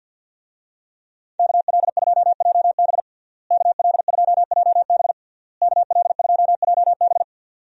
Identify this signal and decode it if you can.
Morse code, heard as a keyed tone, about 52 wpm, 700 Hz. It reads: KL2JB KL2JB KL2JB
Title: USA Call Signs - 5 Characters - Repeated 3x - 50wpm